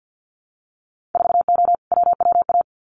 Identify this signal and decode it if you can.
Morse code, heard as a keyed tone, about 50 wpm, 700 Hz. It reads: 4O WWA
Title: Winter Field Day Exchanges 50wpm